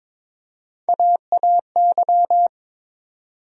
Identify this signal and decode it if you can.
Morse code, heard as a keyed tone, about 21 wpm, 700 Hz. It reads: AAY
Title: USA Call Sign Postfixes - 3 Letters 22wpm